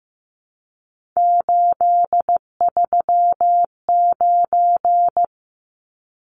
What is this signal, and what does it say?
Morse code, heard as a keyed tone, about 15 wpm, 700 Hz. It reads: 839
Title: Numbers - 3-Digit 15wpm